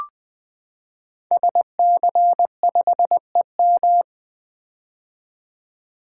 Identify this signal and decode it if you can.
Morse code, heard as a keyed tone, about 20 wpm, 700 Hz. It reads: SC5EM
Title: Code Groups - 5 Letter:Numbers - 20wpm